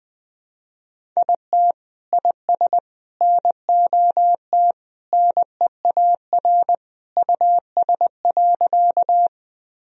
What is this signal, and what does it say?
Morse code, heard as a keyed tone, about 20 wpm, 700 Hz. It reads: IT IS NOT NEAR US.
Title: Sentences from Top 1000 Words - Review Part 1 - 20wpm